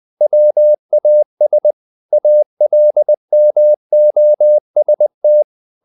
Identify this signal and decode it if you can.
Morse code, heard as a keyed tone, about 20 wpm, 600 Hz. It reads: WAS ALMOST
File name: the old man and the sea / the old man and the sea part 1 20wpm